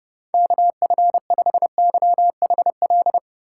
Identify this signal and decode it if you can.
Morse code, heard as a keyed tone, about 32 wpm, 700 Hz. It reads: KF5YHL
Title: USA Call Signs - 6 Characters - 2x Character Spacing - 30wpm